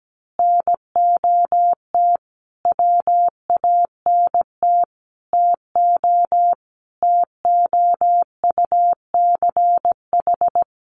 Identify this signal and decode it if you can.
Morse code, heard as a keyed tone, about 17 wpm, 700 Hz. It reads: NOT WANT TO TOUCH
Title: Sentences from Top 1200 Words 17wpm